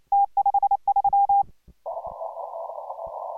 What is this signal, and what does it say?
Morse code, heard as a keyed tone, about 29 wpm, 810 Hz. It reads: T53